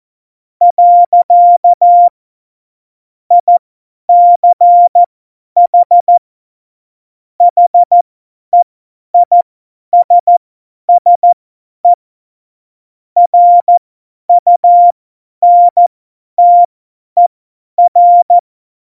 Morse code, heard as a keyed tone, about 14 wpm, spaced as for 10 wpm, 700 Hz. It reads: . ICH HEISSE RUNTER
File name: Morsekurs / Witze / Witz 12. Juni 2018, 50 F70